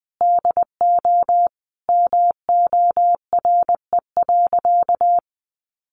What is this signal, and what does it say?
Morse code, heard as a keyed tone, about 19 wpm, 700 Hz. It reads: DO MORE.
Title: Sentences from Top 1400 Words 20wpm